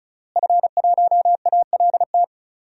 Morse code, heard as a keyed tone, about 36 wpm, 700 Hz. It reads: F1ALT
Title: DXCC-100 Call Signs - 5 Characters - Repeated 3x - 35wpm